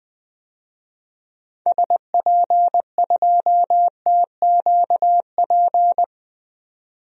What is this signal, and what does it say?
Morse code, heard as a keyed tone, about 21 wpm, 700 Hz. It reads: SP2TQP
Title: DXCC-100 Call Signs - 6 Characters - 20wpm